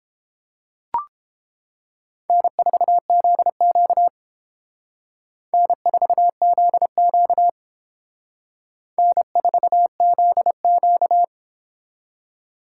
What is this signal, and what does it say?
Morse code, heard as a keyed tone, about 28 wpm, 700 Hz. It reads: N4ZQ N4ZQ N4ZQ